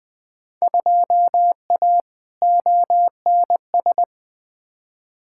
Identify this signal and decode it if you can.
Morse code, heard as a keyed tone, about 20 wpm, 700 Hz. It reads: 2A ONS
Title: Field Day Exchanges Encore - 20wpm